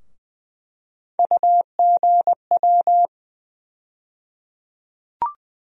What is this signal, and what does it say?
Morse code, heard as a keyed tone, about 20 wpm, 700 Hz. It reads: UGW
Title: USA Call Sign Postfixes - 3 Letters 20wpm